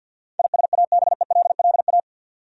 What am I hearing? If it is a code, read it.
Morse code, heard as a keyed tone, about 51 wpm, 700 Hz. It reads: ISABELLA
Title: USA Common Names - 50wpm